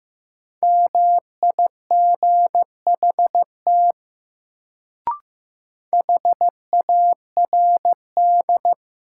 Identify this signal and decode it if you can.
Morse code, heard as a keyed tone, about 16 wpm, 700 Hz. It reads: MIGHT HARD
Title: Sentences from Top 300 Words - New Word Drill - 15wpm